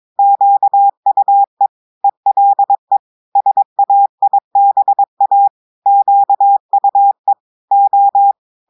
Morse code, heard as a keyed tone, about 22 wpm, 800 Hz. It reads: QUE ELE SAIBA QUE O
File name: audios / portugais / portugais texte en clair 2 vitesse 22 mots minutes